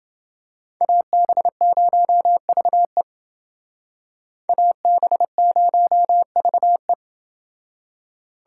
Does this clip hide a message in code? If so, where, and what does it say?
Yes: Morse code from 0.8 to 6.9 s, heard as a keyed tone, about 29 wpm, 700 Hz. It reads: AB0VE AB0VE